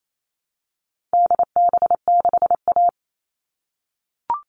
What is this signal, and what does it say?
Morse code, heard as a keyed tone, about 28 wpm, 700 Hz. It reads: DB6A
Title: All Cut Numbers - 4-Digit 28wpm